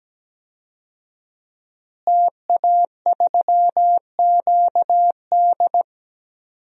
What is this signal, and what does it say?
Morse code, heard as a keyed tone, about 17 wpm, 700 Hz. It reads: TA3QD
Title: DXCC-100 Call Signs - 5 Characters - Repeated 3x - 17wpm